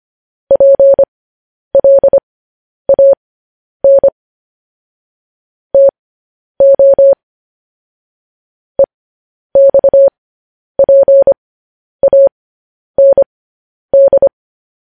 Morse code, heard as a keyed tone, about 25 wpm, spaced as for 10 wpm, 550 Hz. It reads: PLAN TO EXPAND
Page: News Headlines in Morse Code at 10 WPM